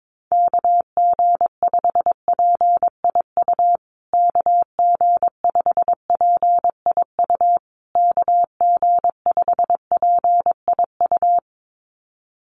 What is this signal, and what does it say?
Morse code, heard as a keyed tone, about 22 wpm, 700 Hz. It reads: KG5PIU KG5PIU KG5PIU
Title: USA Call Signs Repeated 3x 22wpm